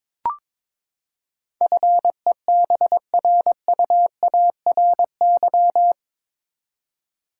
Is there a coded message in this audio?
Yes: Morse code from 1.6 to 5.9 s, heard as a keyed tone, about 22 wpm, 700 Hz. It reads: FEBRUARY